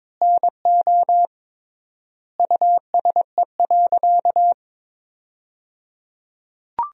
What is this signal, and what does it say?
Morse code, heard as a keyed tone, about 21 wpm, 700 Hz. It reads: NO USE.
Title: 3x Word-Spacing - Sentences from Top 100 Words - 22wpm